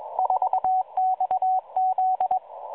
Morse code, heard as a keyed tone, about 21 wpm, 750 Hz, with radio noise. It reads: 4XZ